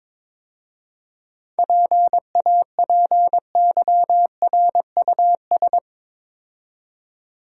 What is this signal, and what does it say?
Morse code, heard as a keyed tone, about 22 wpm, 700 Hz. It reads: PAPYRUS
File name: Easy Words - Speed-Racing with Hints - 22wpm